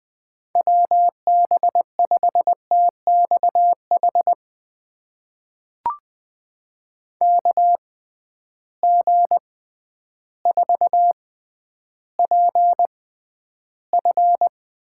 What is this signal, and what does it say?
Morse code, heard as a keyed tone, about 20 wpm, spaced as for 11 wpm, 700 Hz. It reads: WB5TXH KG4PF
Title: USA Call Signs - 6 Characters - 6x Character Spacing - 20wpm